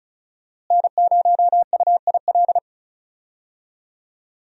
Morse code, heard as a keyed tone, about 36 wpm, 700 Hz. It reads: N0UIL
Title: USA Call Signs - 5 Characters - Repeated 2x - 35wpm